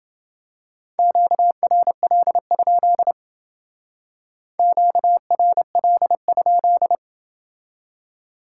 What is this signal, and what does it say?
Morse code, heard as a keyed tone, about 27 wpm, 700 Hz. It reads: QRL? QRL?